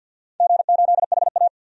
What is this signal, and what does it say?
Morse code, heard as a keyed tone, about 51 wpm, 700 Hz. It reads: G8HA
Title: DXCC-100 Call Signs - 4 Characters - 50wpm